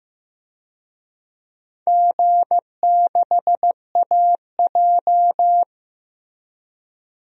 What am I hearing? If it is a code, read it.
Morse code, heard as a keyed tone, about 15 wpm, 700 Hz. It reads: G6AJ
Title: DXCC-100 Call Signs - 4 Characters - 15wpm